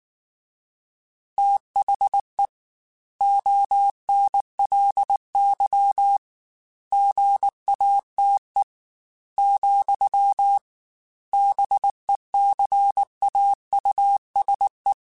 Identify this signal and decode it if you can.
Morse code, heard as a keyed tone, about 19 wpm, 800 Hz. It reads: THE ONLY GATE, BECAUSE
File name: Technical Subjects / Technical Subjects-mp3s / AND Gate @18